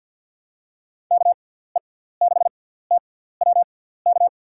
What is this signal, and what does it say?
Morse code, heard as a keyed tone, about 51 wpm, spaced as for 17 wpm, 700 Hz. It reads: KE6TWK
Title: USA Call Signs - 6x Character Spacing - 50wpm